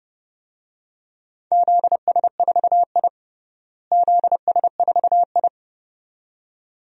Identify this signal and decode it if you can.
Morse code, heard as a keyed tone, about 31 wpm, 700 Hz. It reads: ZS4I ZS4I
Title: DXCC-100 Call Signs - 4 Characters - Repeated 2x - 30wpm